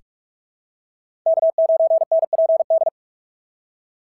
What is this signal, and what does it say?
Morse code, heard as a keyed tone, about 44 wpm, 690 Hz. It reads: K9NPD
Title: SOTA QSO Practice 45wpm